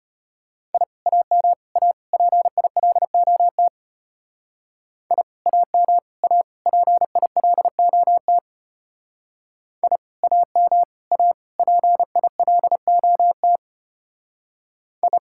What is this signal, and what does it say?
Morse code, heard as a keyed tone, about 31 wpm, 700 Hz. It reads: I AM A PILOT I AM A PILOT I AM A PILOT I
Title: Speed-Racing Common QSO Phrases - 25wpm